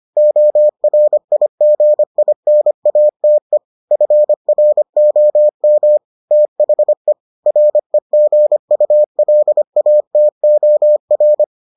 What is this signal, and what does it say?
Morse code, heard as a keyed tone, about 25 wpm, 600 Hz. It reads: ORIGINATE FROM THE REGULATOR